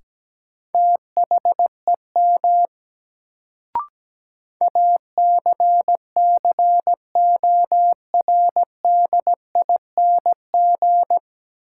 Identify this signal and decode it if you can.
Morse code, heard as a keyed tone, about 17 wpm, 700 Hz. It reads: THEM ACCORDING
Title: Sets of 2 Words - New Word Drill - 17wpm